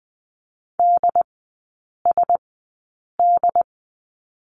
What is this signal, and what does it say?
Morse code, heard as a keyed tone, about 20 wpm, spaced as for 8 wpm, 700 Hz. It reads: DSD
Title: 067 Lesson 12e - Letter d - Three Characters 20wpm